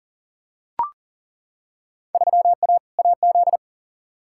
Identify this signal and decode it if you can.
Morse code, heard as a keyed tone, about 40 wpm, 700 Hz. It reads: 3A AZ
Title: Field Day Exchanges - 40wpm